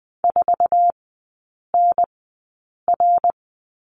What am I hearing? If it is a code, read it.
Morse code, heard as a keyed tone, about 20 wpm, spaced as for 8 wpm, 700 Hz. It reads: 4NR
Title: 060 Lesson 11e - Letter h - Three Characters 20wpm